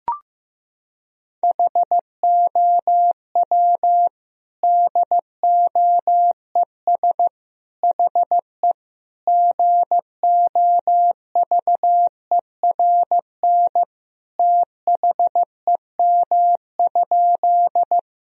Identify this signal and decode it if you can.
Morse code, heard as a keyed tone, about 15 wpm, 700 Hz. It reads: HOW DOES HE GOVERN THEM?
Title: Sentences from Top 1000 Words - Review Part 2 - 15wpm